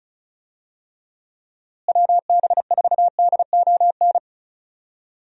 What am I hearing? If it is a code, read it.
Morse code, heard as a keyed tone, about 36 wpm, 700 Hz. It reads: WB4DON
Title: USA Call Signs - 6 Characters - Repeated 2x - 35wpm